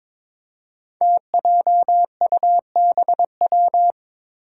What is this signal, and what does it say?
Morse code, heard as a keyed tone, about 21 wpm, 700 Hz. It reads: TJUBW
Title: Five Challenging Letter-Number Combinations 22wpm